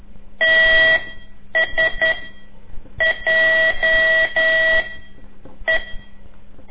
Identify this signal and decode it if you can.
Morse code, heard as a keyed tone, about 11 wpm, 1950 Hz. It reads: T S J E
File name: CORSI CW